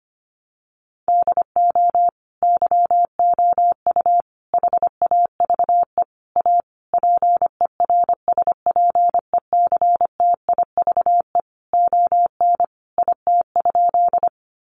Morse code, heard as a keyed tone, about 25 wpm, 700 Hz. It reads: DO YOU HAVE A PERSPECTIVE ON IT?
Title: Sentences from Top 1200 Words 25wpm